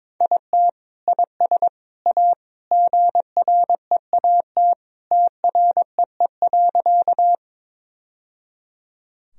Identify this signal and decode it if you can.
Morse code, heard as a keyed tone, about 21 wpm, 700 Hz. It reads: IT IS A GREAT TREE.
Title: Sentences from Top 1000 Words - Review Part 1 - 22wpm